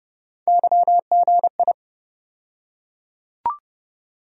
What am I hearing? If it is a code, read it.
Morse code, heard as a keyed tone, about 30 wpm, 700 Hz. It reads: YGI
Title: USA Call Sign Postfixes - 3 Letters 30wpm